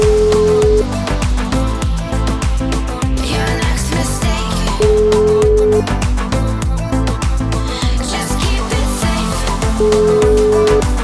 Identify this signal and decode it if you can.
Ringback tone (Europe, 425 Hz), with noise.